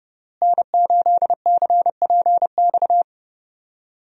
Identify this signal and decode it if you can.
Morse code, heard as a keyed tone, about 31 wpm, 700 Hz. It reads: N8CPX
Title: USA Call Signs - 5 Characters - Repeated 3x - 30wpm